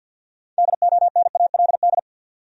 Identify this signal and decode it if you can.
Morse code, heard as a keyed tone, about 46 wpm, 700 Hz. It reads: DONALD
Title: USA Common Names - 50wpm